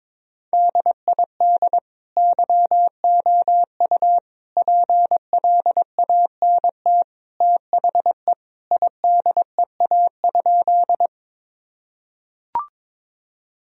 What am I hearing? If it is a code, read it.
Morse code, heard as a keyed tone, about 22 wpm, 700 Hz. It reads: DID YOU PLANT THE IDEA?
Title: Sentences from Top 1000 Words - Review Part 1 - 22wpm